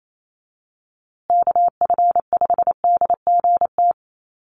Morse code, heard as a keyed tone, about 28 wpm, 700 Hz. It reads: KF5DGT
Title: USA Call Signs Repeated 3x 28wpm